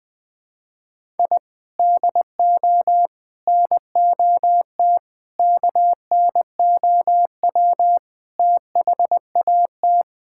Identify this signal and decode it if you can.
Morse code, heard as a keyed tone, about 20 wpm, 700 Hz. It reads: I DO NOT KNOW THAT
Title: Sentences with 5-Word Phrases - 20wpm